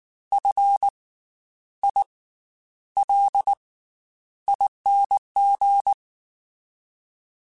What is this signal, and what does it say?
Morse code, heard as a keyed tone, about 19 wpm, spaced as for 10 wpm, 800 Hz. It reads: FILING
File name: Special INGs / Special INGs-mp3s / Slow ing @18